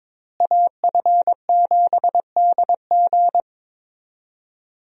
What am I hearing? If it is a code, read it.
Morse code, heard as a keyed tone, about 21 wpm, 700 Hz. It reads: AF7DG
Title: USA Call Signs - 5 Characters - Repeated 2x - 22wpm